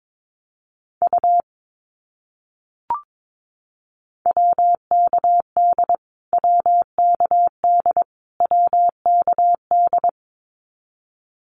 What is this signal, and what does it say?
Morse code, heard as a keyed tone, about 22 wpm, 700 Hz. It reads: U WKD WKD WKD
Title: Abbreviations Repeated 3x - 22wpm